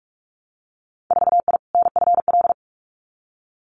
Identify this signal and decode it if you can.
Morse code, heard as a keyed tone, about 46 wpm, 700 Hz. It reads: 4I NFL
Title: Winter Field Day Exchanges 45wpm